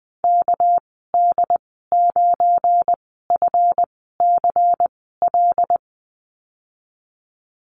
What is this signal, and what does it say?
Morse code, heard as a keyed tone, about 20 wpm, spaced as for 14 wpm, 700 Hz. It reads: KD9FCL
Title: 231 Lesson 37g - Letter x - Call Signs 20wpm